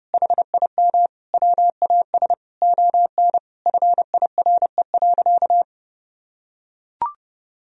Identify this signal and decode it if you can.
Morse code, heard as a keyed tone, about 32 wpm, 700 Hz. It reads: HIM WAS ON FIRE.
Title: Sentences from Top 1000 Words - Review Part 3 - 30wpm